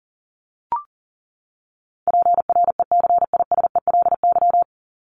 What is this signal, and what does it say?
Morse code, heard as a keyed tone, about 40 wpm, 700 Hz. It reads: PRECISELY
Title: LY Suffix - Top 100 Words with Base Words 40wpm